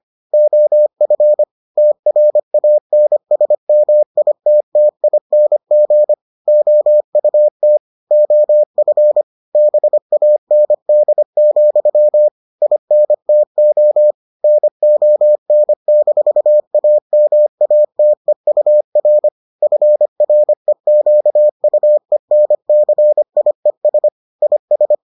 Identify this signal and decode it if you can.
Morse code, heard as a keyed tone, about 25 wpm, 600 Hz. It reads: OF TRANSMITTING OUT OF BAND, INTO NON-AMATEUR FREQUENCIES IS